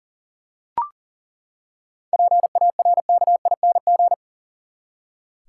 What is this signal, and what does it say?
Morse code, heard as a keyed tone, about 40 wpm, 700 Hz. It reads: PARKING
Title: Seven Letter Words - 40wpm